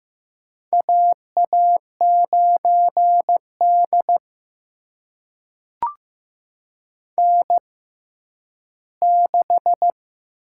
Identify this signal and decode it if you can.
Morse code, heard as a keyed tone, about 15 wpm, spaced as for 10 wpm, 700 Hz. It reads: AA9D N6
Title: USA Call Signs - 6x Character Spacing - 15wpm